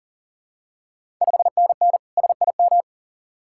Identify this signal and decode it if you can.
Morse code, heard as a keyed tone, about 38 wpm, 690 Hz. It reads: 5NN SIM
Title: Ontario QSO Party Exchange 40wpm